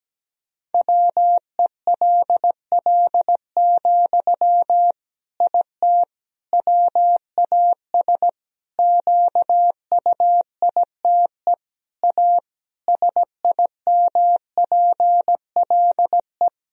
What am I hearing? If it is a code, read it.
Morse code, heard as a keyed tone, about 17 wpm, 700 Hz. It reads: WELL, IT WAS QUITE A SIMPLE
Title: Sentences from Top 1000 Words - Review Part 3 - 17wpm